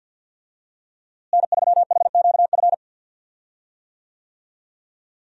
Morse code, heard as a keyed tone, about 51 wpm, 700 Hz. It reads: N3HQF
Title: USA Call Signs - 5 Characters - Repeated 2x - 50wpm